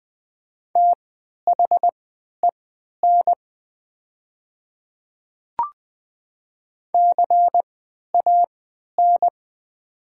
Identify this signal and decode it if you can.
Morse code, heard as a keyed tone, about 20 wpm, spaced as for 11 wpm, 700 Hz. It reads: THEN CAN
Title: 3x Character Spacing - Top 100 Words - 20wpm